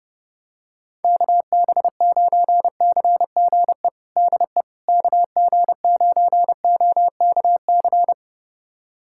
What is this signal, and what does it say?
Morse code, heard as a keyed tone, about 30 wpm, 700 Hz. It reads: KB9CGE DE KG9OKC
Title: USA Callsign DE Callsign - 6 Characters - 30wpm